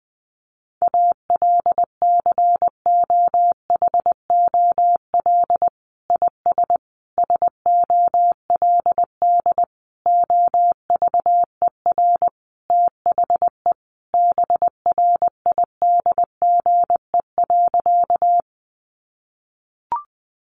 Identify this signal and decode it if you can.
Morse code, heard as a keyed tone, about 20 wpm, 700 Hz. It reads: ALCOHOL IS SOLD OVER THE BRIDGE.
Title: Sentences from Top 1400 Words 20wpm